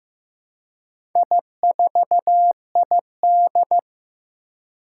Morse code, heard as a keyed tone, about 15 wpm, 700 Hz. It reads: I4ID
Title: Four Challenging Letter-Number Combinations 15wpm